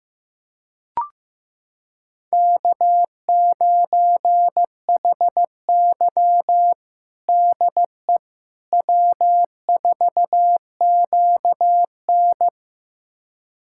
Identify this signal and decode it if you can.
Morse code, heard as a keyed tone, about 15 wpm, 700 Hz. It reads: K9HY DE W4QN
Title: USA Callsign DE Callsign - 4 Characters - 15wpm